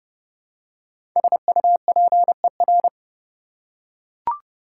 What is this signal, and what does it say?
Morse code, heard as a keyed tone, about 31 wpm, 700 Hz. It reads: SUPER
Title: Prefixes and Suffixes 30wpm